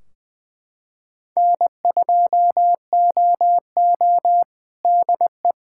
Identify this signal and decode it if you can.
Morse code, heard as a keyed tone, about 21 wpm, 700 Hz. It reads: N2OO DE
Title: K1USN SST QSO Practice 20wpm